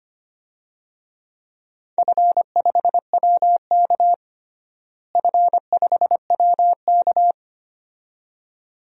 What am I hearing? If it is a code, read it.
Morse code, heard as a keyed tone, about 25 wpm, 700 Hz. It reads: F5WK F5WK
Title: DXCC-100 Call Signs - 4 Characters - Repeated 2x - 25wpm